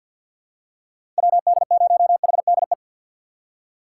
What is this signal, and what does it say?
Morse code, heard as a keyed tone, about 47 wpm, 700 Hz. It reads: WD0HDE
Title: USA Call Signs - 6 Characters - 4x Character Spacing - 50wpm